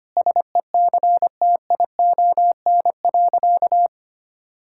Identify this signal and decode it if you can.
Morse code, heard as a keyed tone, about 25 wpm, 700 Hz. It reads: SECTION.
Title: Sentences from Top 1000 Words - Review Part 5 - 25wpm